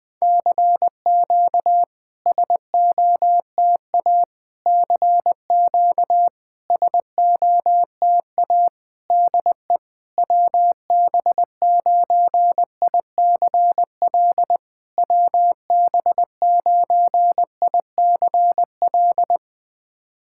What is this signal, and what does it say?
Morse code, heard as a keyed tone, about 20 wpm, 700 Hz. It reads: CQ SOTA CQ SOTA DE WB9ICL WB9ICL
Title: SOTA QSO Practice 20wpm